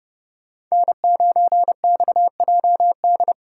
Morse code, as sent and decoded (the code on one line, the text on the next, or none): -. ----. -..- .--- -..
N9XJD